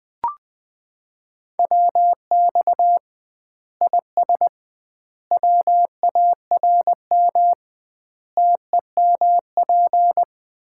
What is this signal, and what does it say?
Morse code, heard as a keyed tone, about 20 wpm, 700 Hz. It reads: WX IS WARM TEMP
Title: Common QSO Phrases - 2x Word-Spacing - 20wpm